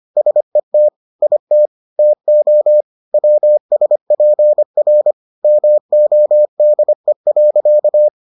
Morse code, heard as a keyed tone, about 25 wpm, 600 Hz. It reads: SET IT TO WSPR MODE.